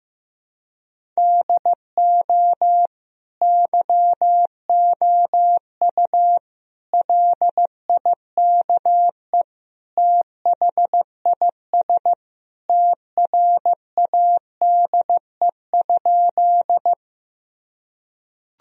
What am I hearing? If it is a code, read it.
Morse code, heard as a keyed tone, about 15 wpm, 700 Hz. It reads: DO YOU LIKE THIS TRADE?
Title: Sentences from Top 1000 Words - Review Part 4 - 15wpm